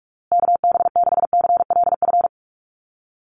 Morse code, heard as a keyed tone, about 44 wpm, 700 Hz. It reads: KB6CLF
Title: USA Call Signs Repeated 3x 45wpm